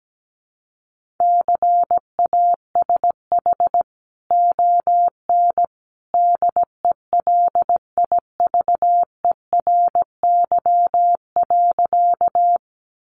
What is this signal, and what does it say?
Morse code, heard as a keyed tone, about 17 wpm, 700 Hz. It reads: CASH ON DELIVERY.
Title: Sentences from Top 1500 Words 17wpm